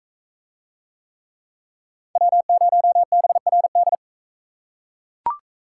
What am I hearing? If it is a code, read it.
Morse code, heard as a keyed tone, about 38 wpm, 700 Hz. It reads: W0BRD